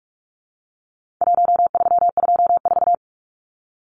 Morse code, heard as a keyed tone, about 44 wpm, 700 Hz. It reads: 1324
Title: Numbers - Contest Style - 4-Digit 45wpm